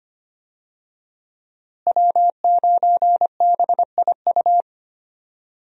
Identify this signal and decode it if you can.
Morse code, heard as a keyed tone, about 25 wpm, 700 Hz. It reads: W9BIU
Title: USA Call Signs - 5 Characters - Repeated 3x - 25wpm